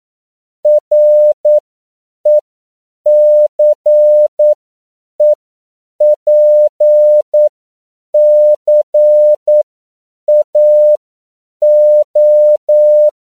Morse code, heard as a keyed tone, about 9 wpm, spaced as for 7 wpm, 600 Hz. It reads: RECEPCAO
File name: CW / 10PPM